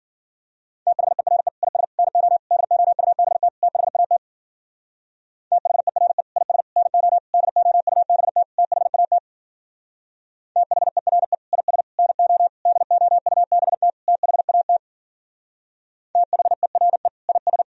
Morse code, heard as a keyed tone, about 52 wpm, 700 Hz. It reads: THERE IS NO DOUBT THAT THERE IS NO DOUBT THAT THERE IS NO DOUBT THAT THERE IS